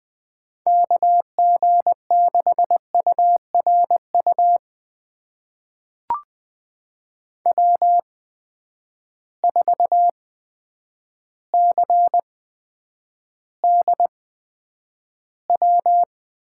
Morse code, heard as a keyed tone, about 20 wpm, spaced as for 9 wpm, 700 Hz. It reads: KG6URU W4CDW
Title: USA Call Signs - 8x Character Spacing - 20wpm